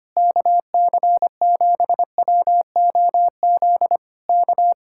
Morse code, heard as a keyed tone, about 26 wpm, 700 Hz. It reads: KC7WOZ K
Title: Speed-Racing Common QSO Phrases - 25wpm